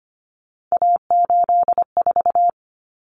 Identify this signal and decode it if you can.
Morse code, heard as a keyed tone, about 25 wpm, 700 Hz. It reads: A84
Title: Common Cut Numbers - 3-Digit 25wpm